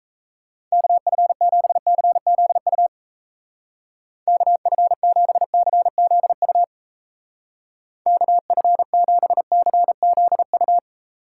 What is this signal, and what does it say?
Morse code, heard as a keyed tone, about 37 wpm, 700 Hz. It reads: KF7CZU KF7CZU KF7CZU